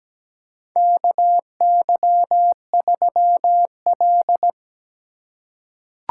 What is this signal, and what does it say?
Morse code, heard as a keyed tone, about 17 wpm, 700 Hz. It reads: KY3L
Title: USA Call Signs - 4 Characters - 2x Character Spacing - 17wpm